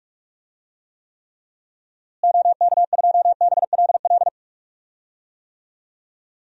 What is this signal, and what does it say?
Morse code, heard as a keyed tone, about 46 wpm, 700 Hz. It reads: OK2BLL
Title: DXCC-100 Call Signs - 6 Characters - Repeated 3x - 45wpm